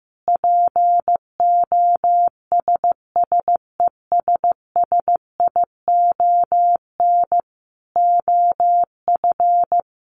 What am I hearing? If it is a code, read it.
Morse code, heard as a keyed tone, about 15 wpm, 700 Hz. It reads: POSSESSION OF
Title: Sentences from Top 1400 Words 15wpm